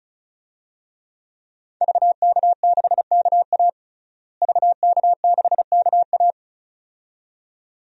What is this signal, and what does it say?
Morse code, heard as a keyed tone, about 36 wpm, 700 Hz. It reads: VK6KA VK6KA
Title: DXCC-100 Call Signs - 5 Characters - Repeated 2x - 35wpm